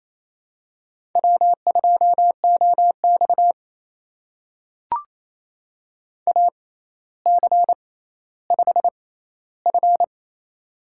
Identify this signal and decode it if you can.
Morse code, heard as a keyed tone, about 28 wpm, spaced as for 14 wpm, 700 Hz. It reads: W2OX AC5F
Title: USA Call Signs - 4 Characters - 6x Character Spacing - 28wpm